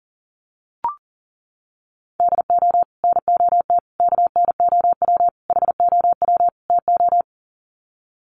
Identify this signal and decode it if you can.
Morse code, heard as a keyed tone, about 38 wpm, 700 Hz. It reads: DO NOT KNOW HOW TO
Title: Sets of 5 Words 40wpm